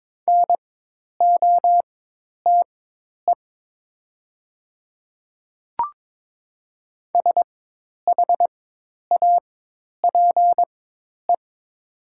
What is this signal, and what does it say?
Morse code, heard as a keyed tone, about 22 wpm, spaced as for 10 wpm, 700 Hz. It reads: NOTE SHAPE
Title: 4x Character Spacing - Top 500 Words - 22wpm